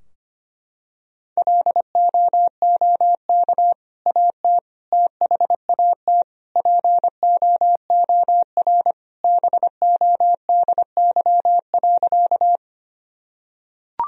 Morse code, heard as a keyed tone, about 25 wpm, 700 Hz. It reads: LOOK AT THAT POOR BODY.
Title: Sentences from Top 1000 Words - Review Part 2 - 25wpm